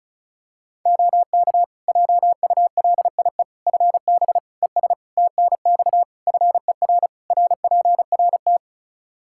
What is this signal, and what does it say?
Morse code, heard as a keyed tone, about 36 wpm, 700 Hz. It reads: OK JULIE FB ES TNX FER RPRT
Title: Practice QSO - Exchange 2 - Info - 35wpm